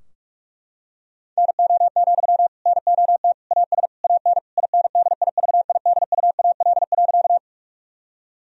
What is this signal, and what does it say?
Morse code, heard as a keyed tone, about 41 wpm, 700 Hz. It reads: NO, NOT AS AN INDIVIDUAL.
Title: Sentences from Top 1000 Words - Review Part 4 - 45wpm